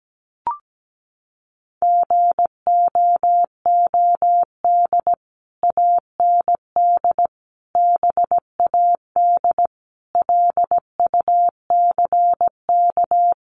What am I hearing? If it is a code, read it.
Morse code, heard as a keyed tone, about 17 wpm, 700 Hz. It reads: GOOD AND BAD LUCK
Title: Sentences from Top 1500 Words 17wpm